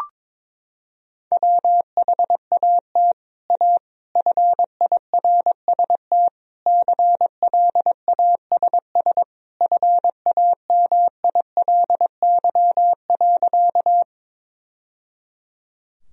Morse code, heard as a keyed tone, about 22 wpm, 700 Hz. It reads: WHAT A FIRST CLASS FAMILY.
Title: Sentences from Top 1000 Words - Review Part 2 - 22wpm